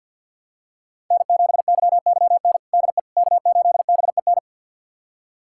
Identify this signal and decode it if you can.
Morse code, heard as a keyed tone, about 52 wpm, 700 Hz. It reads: N7YYN DE K8BEN
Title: USA Callsign DE Callsign - 5 Characters - 50wpm